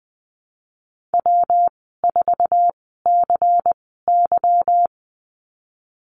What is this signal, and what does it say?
Morse code, heard as a keyed tone, about 20 wpm, spaced as for 15 wpm, 700 Hz. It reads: W4CY
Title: 154 Lesson 25g - Letter p - Call Signs 20wpm